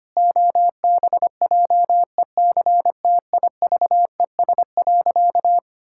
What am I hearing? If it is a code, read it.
Morse code, heard as a keyed tone, about 25 wpm, 700 Hz. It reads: OBJECTIVES.
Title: Pangram Challenge v2 - 25wpm